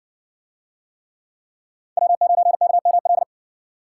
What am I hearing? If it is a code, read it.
Morse code, heard as a keyed tone, about 60 wpm, 700 Hz. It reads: W9KML